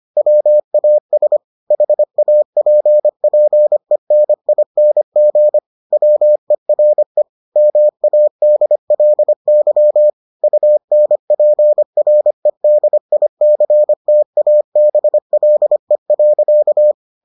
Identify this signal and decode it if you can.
Morse code, heard as a keyed tone, about 25 wpm, 600 Hz. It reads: WAS HAPPENING WERE MADLY UNPREDICTABLE.